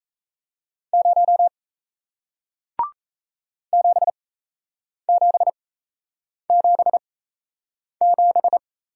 Morse code, heard as a keyed tone, about 38 wpm, spaced as for 8 wpm, 700 Hz. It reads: 0 7777